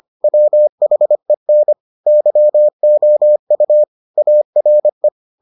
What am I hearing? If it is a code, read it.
Morse code, heard as a keyed tone, about 25 wpm, 600 Hz. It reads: WHEN YOU ARE